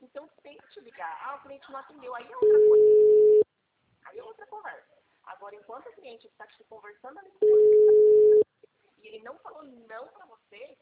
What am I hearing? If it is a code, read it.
Ringback tone (Europe, 425 Hz), with speech.